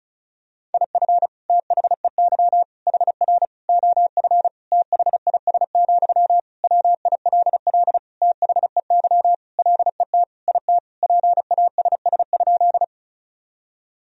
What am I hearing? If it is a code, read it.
Morse code, heard as a keyed tone, about 36 wpm, 700 Hz. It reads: IF THEY HR OF THIS, WILL THEY LET IT PASS?
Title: Sentences from Top 400 Words and Abbreviations 35wpm